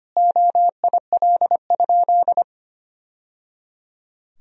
Morse code, heard as a keyed tone, about 25 wpm, 700 Hz. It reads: OIL?
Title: Sentences from Top 1000 Words - Review Part 4 - 25wpm